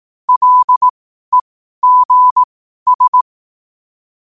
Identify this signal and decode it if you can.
Morse code, heard as a keyed tone, about 18 wpm, spaced as for 12 wpm, 1000 Hz. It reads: LEGS
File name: qotdfiles / 12wpm